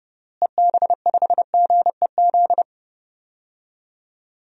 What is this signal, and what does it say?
Morse code, heard as a keyed tone, about 32 wpm, 700 Hz. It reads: EB5GEZ
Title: DXCC-100 Call Signs - 6 Characters - Repeated 2x - 30wpm